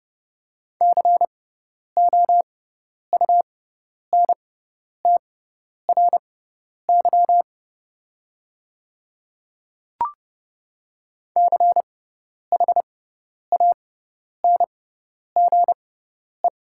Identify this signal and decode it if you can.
Morse code, heard as a keyed tone, about 30 wpm, spaced as for 10 wpm, 700 Hz. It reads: COUNTRY CHANGE
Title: 6x Character Spacing - Top 300 Words - 30wpm